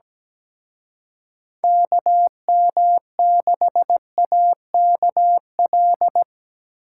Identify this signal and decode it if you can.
Morse code, heard as a keyed tone, about 20 wpm, 700 Hz. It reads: KM6AKL
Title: Speed Racing - USA 2x3 Call Signs 17wpm